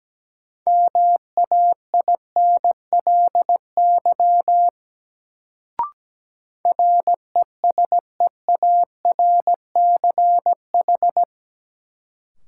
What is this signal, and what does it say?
Morse code, heard as a keyed tone, about 17 wpm, 700 Hz. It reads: MAINLY RESEARCH
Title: Sentences from Top 1400 Words - New Word Drill Alt - 17wpm